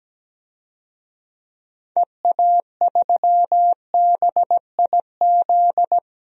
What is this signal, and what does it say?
Morse code, heard as a keyed tone, about 17 wpm, 700 Hz. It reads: EA3BIZ
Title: DXCC-100 Call Signs - 6 Characters - 17wpm